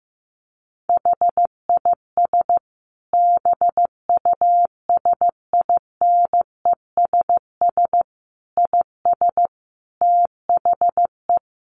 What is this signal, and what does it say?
Morse code, heard as a keyed tone, about 15 wpm, 700 Hz. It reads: HIS BUSINESS IS THE
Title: Sentences from Top 600 Words 15wpm